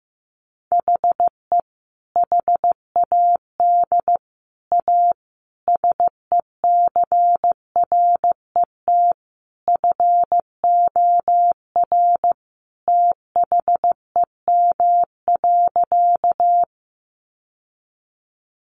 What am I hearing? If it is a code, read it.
Morse code, heard as a keyed tone, about 15 wpm, 700 Hz. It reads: HE HAD A SECRET FOR THEM.
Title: Sentences from Top 1200 Words 15wpm